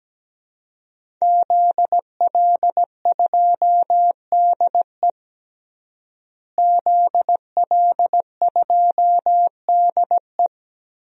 Morse code, heard as a keyed tone, about 17 wpm, 700 Hz. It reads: ZL2DE ZL2DE
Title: DXCC-100 Call Signs - 5 Characters - Repeated 3x - 17wpm